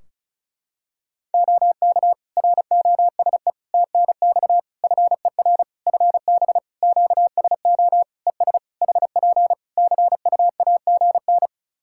Morse code, heard as a keyed tone, about 36 wpm, 700 Hz. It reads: OK ROSE TNX FER FB QSO ES HP CUAGN
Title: Practice QSO - Exchange 4 - Ending - 35wpm